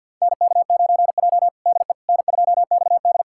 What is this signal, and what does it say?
Morse code, heard as a keyed tone, about 51 wpm, 700 Hz. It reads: NK9J DE N2XD
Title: USA Callsign DE Callsign - 4 Characters - 50wpm